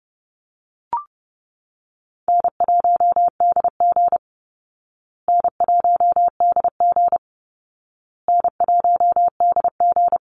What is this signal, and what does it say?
Morse code, heard as a keyed tone, about 30 wpm, 700 Hz. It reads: N1DG N1DG N1DG
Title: Call Signs Repeated 2x - 30wpm